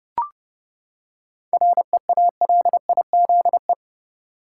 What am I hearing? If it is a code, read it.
Morse code, heard as a keyed tone, about 30 wpm, 700 Hz. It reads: REALIZE
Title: Seven Letter Words - 30wpm